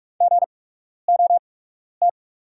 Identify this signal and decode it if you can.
Morse code, heard as a keyed tone, about 45 wpm, spaced as for 12 wpm, 690 Hz. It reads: GOT
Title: 8x Character Spacing - Top 300 Words - 45wpm